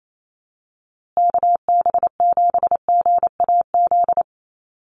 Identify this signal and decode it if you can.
Morse code, heard as a keyed tone, about 28 wpm, 700 Hz. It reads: KB7GAZ
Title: USA Call Signs Repeated 2x 28wpm